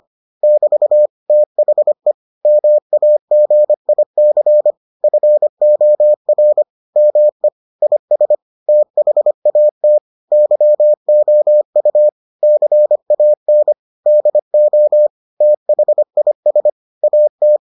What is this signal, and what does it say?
Morse code, heard as a keyed tone, about 25 wpm, 600 Hz. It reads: <BT> THE MAGIC FOR ME IS THAT YOU CAN DO THIS AT